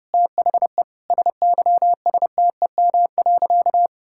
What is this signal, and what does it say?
Morse code, heard as a keyed tone, about 31 wpm, 700 Hz. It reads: THE SYSTEM.
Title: Sentences from Top 1000 Words - Review Part 6 - 30wpm